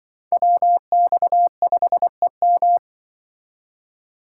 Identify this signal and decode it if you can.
Morse code, heard as a keyed tone, about 24 wpm, 700 Hz. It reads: WX5EM
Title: Speed Racing - USA Call Signs 20wpm